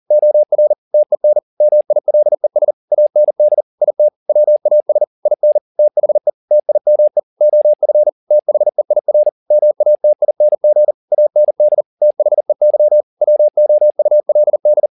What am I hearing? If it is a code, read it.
Morse code, heard as a keyed tone, about 40 wpm, 600 Hz. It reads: OR TEN MILES AND IT WAS IN THE TIME OF THEIR MATING AND THEY WOULD
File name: the old man and the sea 2 40wpm